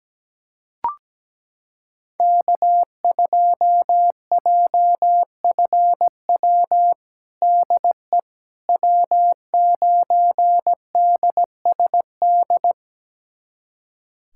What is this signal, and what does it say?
Morse code, heard as a keyed tone, about 17 wpm, 700 Hz. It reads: K2JFW DE W9DSD
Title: USA Callsign DE Callsign - 5 Characters - 17wpm